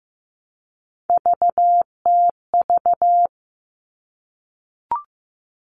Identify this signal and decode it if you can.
Morse code, heard as a keyed tone, about 15 wpm, 700 Hz. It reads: VTV
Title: All Cut Numbers - 3-Digit 15wpm